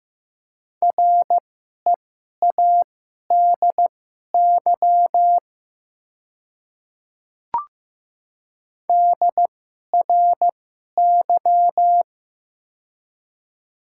Morse code, heard as a keyed tone, about 15 wpm, spaced as for 11 wpm, 700 Hz. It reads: READY DRY
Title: 2x Character Spacing - Top 500 Words - 15wpm